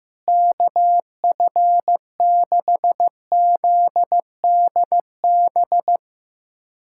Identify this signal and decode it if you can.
Morse code, heard as a keyed tone, about 15 wpm, 700 Hz. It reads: KF6ZDB
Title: USA Callsign DE Callsign - 6 Characters - 15wpm